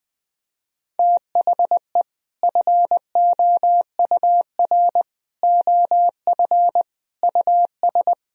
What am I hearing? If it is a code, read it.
Morse code, heard as a keyed tone, about 19 wpm, 700 Hz. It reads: THE FOUR OF US
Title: Sentences from Top 1000 Words - Review Part 1 - 20wpm